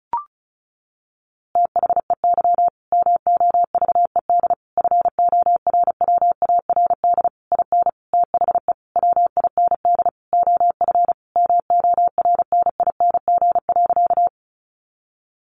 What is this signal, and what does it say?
Morse code, heard as a keyed tone, about 34 wpm, 700 Hz. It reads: THEY MOVED FORWARD IN THE WIND OF MORNING.
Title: A Sound Of Thunder 35wpm